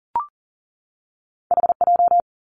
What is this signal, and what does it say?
Morse code, heard as a keyed tone, about 38 wpm, 700 Hz. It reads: HJ
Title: Two Challenging Letter-Number Combinations 40wpm